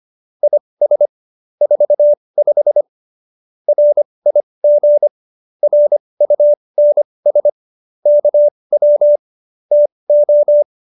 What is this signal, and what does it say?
Morse code, heard as a keyed tone, about 25 wpm, spaced as for 20 wpm, 590 Hz. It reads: IS 45 RIG RUNS KW TO
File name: QSO301 20